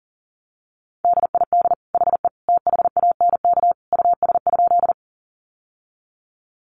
Morse code, heard as a keyed tone, about 40 wpm, 700 Hz. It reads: DID HE THANK US?
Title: Sentences from Top 600 Words 40wpm